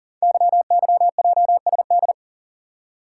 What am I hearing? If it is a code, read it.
Morse code, heard as a keyed tone, about 39 wpm, 690 Hz. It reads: YYJSD
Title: Code Groups - 5 Letters - 40wpm